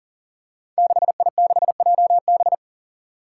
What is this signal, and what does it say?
Morse code, heard as a keyed tone, about 40 wpm, 700 Hz. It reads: 6I6JB
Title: Five Challenging Letter-Number Combinations 40wpm